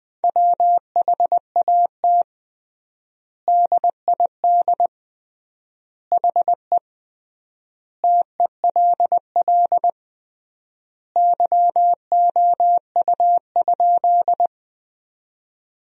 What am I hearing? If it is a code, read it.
Morse code, heard as a keyed tone, about 21 wpm, 700 Hz. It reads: WHAT DID HE TELL YOU?
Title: 3x Word-Spacing - Sentences from Top 200 Words - 20wpm